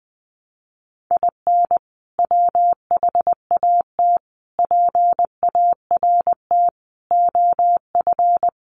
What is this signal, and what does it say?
Morse code, heard as a keyed tone, about 20 wpm, 700 Hz. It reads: IN WHAT PART OF
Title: Sentences from Top 200 Words 20wpm